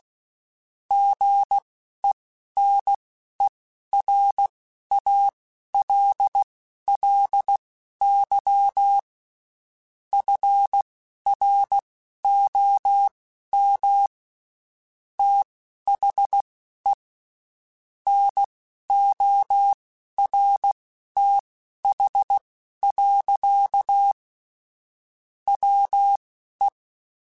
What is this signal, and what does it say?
Morse code, heard as a keyed tone, about 16 wpm, spaced as for 11 wpm, 800 Hz. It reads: GENERALLY FROM THE NORTH. WE
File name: Spaced Characters / Spaced Characters-mp3s / Weather4 24 sp @15